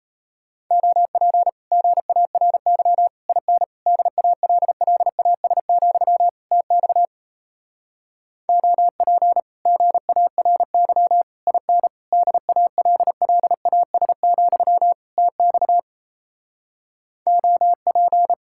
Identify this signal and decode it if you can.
Morse code, heard as a keyed tone, about 37 wpm, 700 Hz. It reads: OP GARY IN DALLAS, TX OP GARY IN DALLAS, TX OP